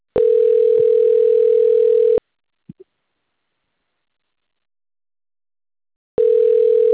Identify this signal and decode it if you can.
Ringback tone (North America), 440 + 480 Hz.